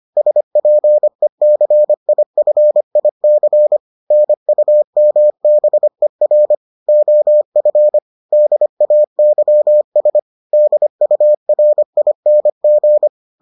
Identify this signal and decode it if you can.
Morse code, heard as a keyed tone, about 25 wpm, 600 Hz. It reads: SPECIFIC NUMBER OF DAYS DURING